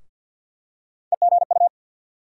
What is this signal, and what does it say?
Morse code, heard as a keyed tone, about 47 wpm, 690 Hz. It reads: EGU
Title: USA Call Sign Postfixes - 3 Letters 50wpm